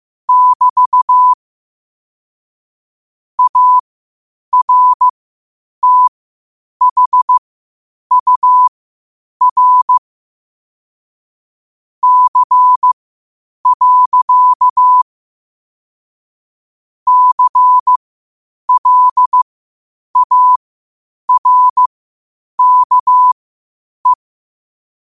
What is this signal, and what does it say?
Morse code, heard as a keyed tone, about 15 wpm, spaced as for 8 wpm, 1000 Hz. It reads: <BT> ARTHUR C. CLARKE
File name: qotdfiles / 8wpm